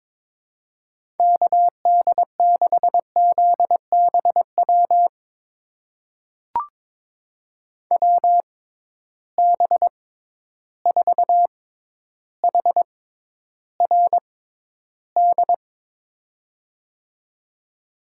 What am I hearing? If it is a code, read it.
Morse code, heard as a keyed tone, about 22 wpm, spaced as for 11 wpm, 700 Hz. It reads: KD6ZBW WB4HRD
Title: USA Call Signs - 6 Characters - 6x Character Spacing - 22wpm